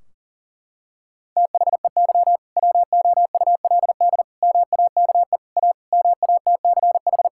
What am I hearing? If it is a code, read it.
Morse code, heard as a keyed tone, about 40 wpm, 700 Hz. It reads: THEY WOULD MAKE A MATCH
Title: Sentences from Top 1600 Words 40wpm